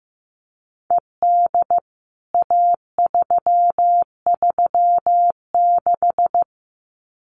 Morse code, heard as a keyed tone, about 15 wpm, 700 Hz. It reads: ED A336
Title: CWT Exchange 15wpm